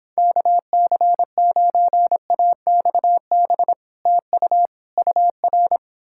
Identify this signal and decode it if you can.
Morse code, heard as a keyed tone, about 29 wpm, 700 Hz. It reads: KC9AXB TU UR